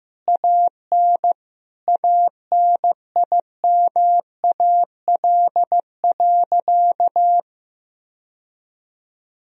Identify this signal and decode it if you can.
Morse code, heard as a keyed tone, about 15 wpm, 700 Hz. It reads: AN ANIMAL.
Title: Sentences from Top 1000 Words - Review Part 1 - 15wpm